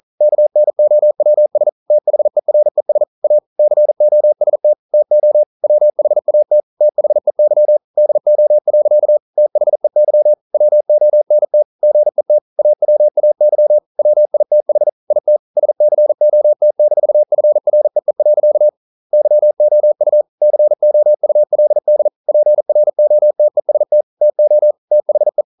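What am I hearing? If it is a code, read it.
Morse code, heard as a keyed tone, about 41 wpm, 600 Hz. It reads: KNOWS THERES A COST TO WHAT THEY DO. THEY WONT GET AWAY WITH IT SCOT-FREE. YOU COULD PROTEST TO THE
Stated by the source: Morse code practice at 41 wpm, from CW med Gnister.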